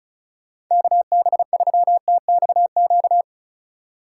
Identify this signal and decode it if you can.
Morse code, heard as a keyed tone, about 44 wpm, 700 Hz. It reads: KB3TXQ